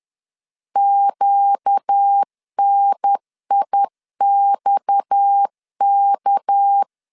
Morse code, heard as a keyed tone, about 10 wpm, 800 Hz. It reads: QNIXK